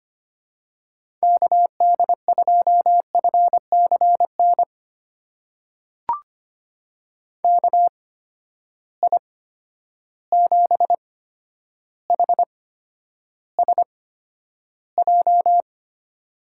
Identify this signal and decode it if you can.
Morse code, heard as a keyed tone, about 25 wpm, spaced as for 11 wpm, 700 Hz. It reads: KD2FCN KI7HSJ
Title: USA Call Signs - 6 Characters - 8x Character Spacing - 25wpm